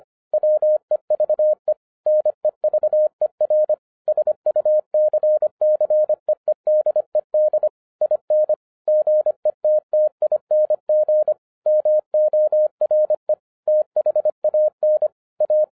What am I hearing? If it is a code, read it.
Morse code, heard as a keyed tone, about 24 wpm, 610 Hz. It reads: WEVE NEVER SUCCEEDED IN GETTING MORE THAN A
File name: 2012-01-mp3 / n14-FUN WITH NUMBERS-25wpm